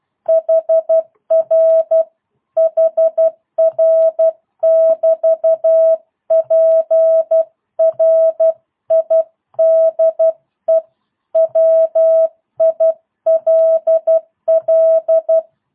Morse code, heard as a keyed tone, about 12 wpm, 650 Hz. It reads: HR HR <BT> PRIDE WILL
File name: jh8cbh / 2022cw